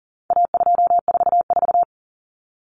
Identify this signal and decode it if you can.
Morse code, heard as a keyed tone, about 40 wpm, 700 Hz. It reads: A244
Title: Common Cut Numbers - 4-Digit 40wpm